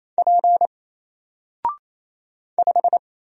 Morse code, heard as a keyed tone, about 27 wpm, 700 Hz. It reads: P 5
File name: Single-Letter-Number-28wpm